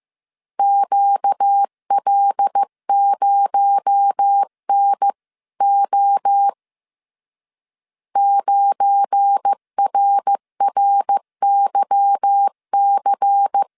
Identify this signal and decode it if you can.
Morse code, heard as a keyed tone, about 15 wpm, 800 Hz. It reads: QL0NO 9RRYC
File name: CW / DET / 1 9 1500C S A 14wpm 3222